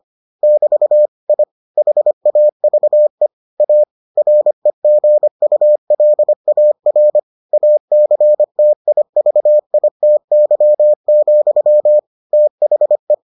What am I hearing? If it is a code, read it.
Morse code, heard as a keyed tone, about 25 wpm, 600 Hz. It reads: <BT> I HAVE A REGULAR ACTIVITY, THE